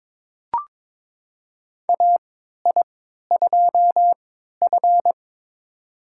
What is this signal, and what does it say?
Morse code, heard as a keyed tone, about 22 wpm, spaced as for 12 wpm, 700 Hz. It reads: AI2F
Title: USA Call Signs - 4 Characters - 3x Character Spacing - 22wpm